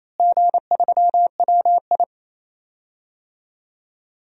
Morse code, heard as a keyed tone, about 28 wpm, 700 Hz. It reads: G3WI
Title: DXCC-100 Call Signs - 4 Characters - Repeated 2x - 28wpm